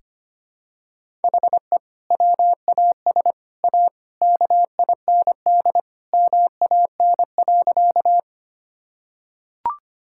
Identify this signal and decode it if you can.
Morse code, heard as a keyed tone, about 25 wpm, 700 Hz. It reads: HE WAS A KIND MAN.
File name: practice / 25wpm / Sentences with 3-Word Phrases - 25wpm